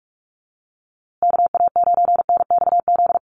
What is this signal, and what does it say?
Morse code, heard as a keyed tone, about 44 wpm, 700 Hz. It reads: KA9NXZ
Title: USA Call Signs Repeated 2x 45wpm